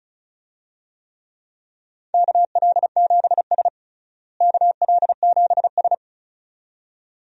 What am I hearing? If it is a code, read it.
Morse code, heard as a keyed tone, about 36 wpm, 700 Hz. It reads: KL7S KL7S
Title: DXCC-100 Call Signs - 4 Characters - Repeated 2x - 35wpm